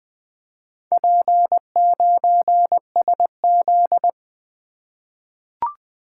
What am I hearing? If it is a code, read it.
Morse code, heard as a keyed tone, about 20 wpm, 700 Hz. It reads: P9SZ
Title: Four Letter-Number Combinations - 20wpm